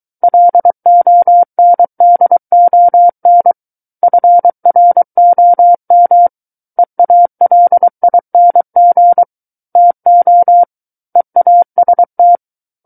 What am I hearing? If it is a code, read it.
Morse code, heard as a keyed tone, about 23 wpm, 700 Hz. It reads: LONDON FROM EALING TO EAST
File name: Book One / 14-Chapter 14 - 23 WPM